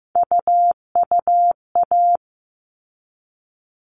Morse code, heard as a keyed tone, about 15 wpm, 700 Hz. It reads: UUA
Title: All Cut Numbers - 3-Digit 15wpm